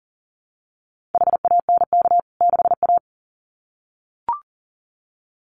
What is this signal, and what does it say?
Morse code, heard as a keyed tone, about 40 wpm, 700 Hz. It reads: HANK 6A
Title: CWT Exchange 40wpm